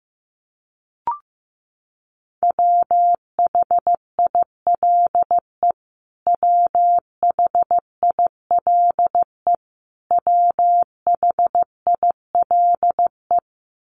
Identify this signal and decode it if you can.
Morse code, heard as a keyed tone, about 15 wpm, 700 Hz. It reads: WHILE WHILE WHILE
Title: Top 300 Words Repeated 15wpm